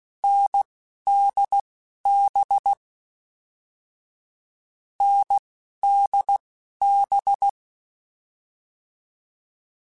Morse code, heard as a keyed tone, about 16 wpm, spaced as for 11 wpm, 800 Hz. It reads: NDB NDB
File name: Short Runs / Short Runs-mp3s / NDB NDB6 @10